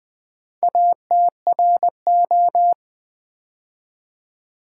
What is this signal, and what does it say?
Morse code, heard as a keyed tone, about 21 wpm, 700 Hz. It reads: ATRO
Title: Build A Word - 2 Letters At A Time - 8-Letter Words - 20wpm